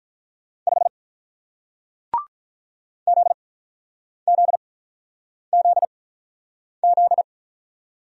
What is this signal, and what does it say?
Morse code, heard as a keyed tone, about 47 wpm, spaced as for 9 wpm, 700 Hz. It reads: 5 ZZZZ